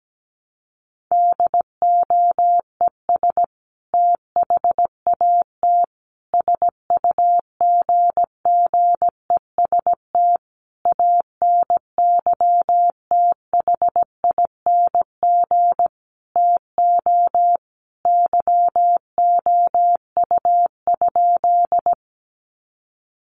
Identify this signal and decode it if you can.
Morse code, heard as a keyed tone, about 17 wpm, 700 Hz. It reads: DOES THAT SUGGEST ANYTHING TO YOU?
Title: Sentences from Top 1300 Words 17wpm